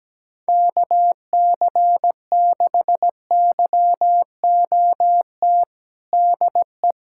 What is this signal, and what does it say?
Morse code, heard as a keyed tone, about 17 wpm, 700 Hz. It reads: KC6YOT DE
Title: USA Callsign DE Callsign 17wpm